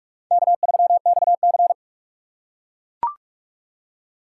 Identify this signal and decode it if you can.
Morse code, heard as a keyed tone, about 40 wpm, 700 Hz. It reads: K3XC